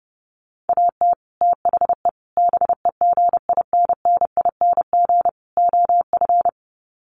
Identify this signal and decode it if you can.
Morse code, heard as a keyed tone, about 30 wpm, 700 Hz. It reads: AT THE BEGINNING OF
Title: Sets of 4 Words 30wpm